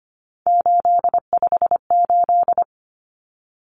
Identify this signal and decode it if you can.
Morse code, heard as a keyed tone, about 25 wpm, 700 Hz. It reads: 858
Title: Numbers - 3-Digit 25wpm